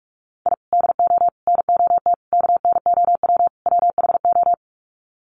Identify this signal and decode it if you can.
Morse code, heard as a keyed tone, about 44 wpm, 700 Hz. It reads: I DO NOT KNOW WHO
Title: Sets of 5 Words 45wpm